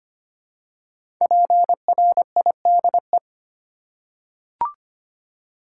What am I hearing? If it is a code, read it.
Morse code, heard as a keyed tone, about 25 wpm, 700 Hz. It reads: PRIDE
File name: 5 Letter Words - First 3 Letters Given - 25wpm